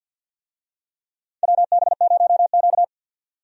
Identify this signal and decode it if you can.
Morse code, heard as a keyed tone, about 53 wpm, 700 Hz. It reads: WB0Q
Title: USA Call Signs - 4 Characters - 50wpm